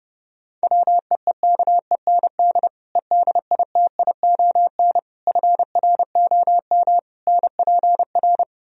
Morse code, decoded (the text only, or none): WEEKEND EDITION FROM NPR